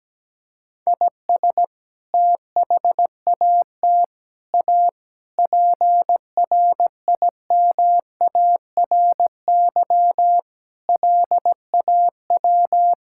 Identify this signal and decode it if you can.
Morse code, heard as a keyed tone, about 17 wpm, 700 Hz. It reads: IS THAT A PRIMARY LAW
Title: Sentences from Top 1000 Words - Review Part 5 - 17wpm